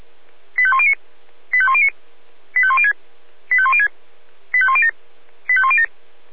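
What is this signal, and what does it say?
Five-tone selective call (ZVEI), heard as quick runs of five tones, tones of about 70 ms. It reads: 85109 85109 85106 85106 85107 85107
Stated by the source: ZVEI five-tone selective call received by SDR.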